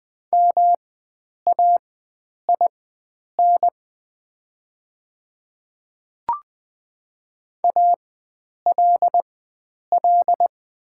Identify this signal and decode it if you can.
Morse code, heard as a keyed tone, about 20 wpm, spaced as for 9 wpm, 700 Hz. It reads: MAIN ALL
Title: 4x Character Spacing - Top 500 Words - 20wpm